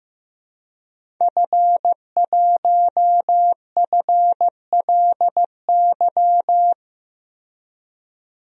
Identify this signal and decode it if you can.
Morse code, heard as a keyed tone, about 16 wpm, 700 Hz. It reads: F1FLY
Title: DXCC-100 Call Signs - 5 Characters - Repeated 3x - 15wpm